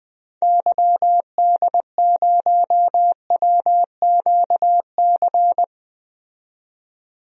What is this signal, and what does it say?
Morse code, heard as a keyed tone, about 20 wpm, 700 Hz. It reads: YD0WQC
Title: DXCC-100 Call Signs - 6 Characters - 20wpm